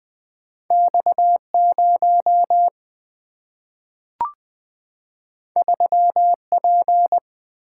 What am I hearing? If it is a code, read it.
Morse code, heard as a keyed tone, about 20 wpm, 700 Hz. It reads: X 0 3 P
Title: Single-Letter Single-Number 20wpm